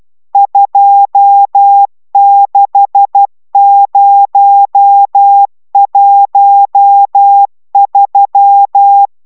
Morse code, heard as a keyed tone, about 12 wpm, 800 Hz. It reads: 26013
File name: taviro tanulas / cw 69 22 10